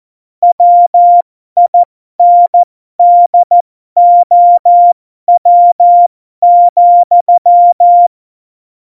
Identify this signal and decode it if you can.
Morse code, heard as a keyed tone, about 14 wpm, spaced as for 12 wpm, 700 Hz. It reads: WINDOW,
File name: Morsekurs / Witze